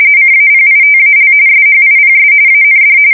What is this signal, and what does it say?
RTTY, about 45.45 baud. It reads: CAPITAL SHIPS AND
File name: fsk170 / battleshipnj